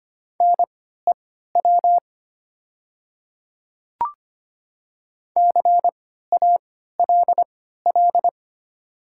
Morse code, heard as a keyed tone, about 25 wpm, spaced as for 14 wpm, 700 Hz. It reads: NEW CALL
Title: 3x Character Spacing - Top 500 Words - 25wpm